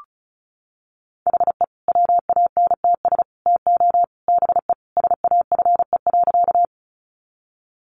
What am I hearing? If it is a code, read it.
Morse code, heard as a keyed tone, about 34 wpm, 700 Hz. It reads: HE WANTS TO BE SAFE.
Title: Additional Easy Phrases 35wpm